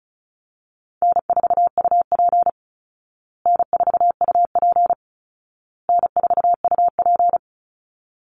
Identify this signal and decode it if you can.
Morse code, heard as a keyed tone, about 34 wpm, 700 Hz. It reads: N4UP N4UP N4UP
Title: Call Signs Repeated 2x - 35wpm